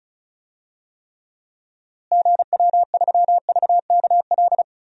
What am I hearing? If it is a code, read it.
Morse code, heard as a keyed tone, about 34 wpm, 700 Hz. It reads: GW3VKL
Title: DXCC-100 Call Signs - 6 Characters - Repeated 3x - 35wpm